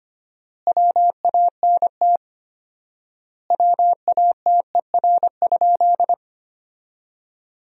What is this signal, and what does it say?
Morse code, heard as a keyed tone, about 25 wpm, 700 Hz. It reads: WANT WATER?
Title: 4x Word-Spacing - Sentences from Top 200 Words - 25wpm